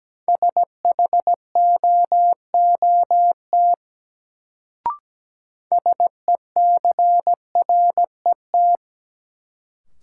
Morse code, heard as a keyed tone, about 17 wpm, 700 Hz. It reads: SHOOT SECRET
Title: Sentences from Top 1200 Words - New Word Drill Alt - 17wpm